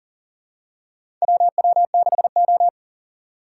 Morse code, heard as a keyed tone, about 40 wpm, 700 Hz. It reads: WW6O
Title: USA Call Signs - 4 Characters - 3x Character Spacing - 40wpm